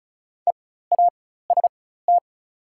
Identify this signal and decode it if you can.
Morse code, heard as a keyed tone, about 34 wpm, spaced as for 16 wpm, 690 Hz. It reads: EAST
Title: 4x Character Spacing - Top 500 Words - 35wpm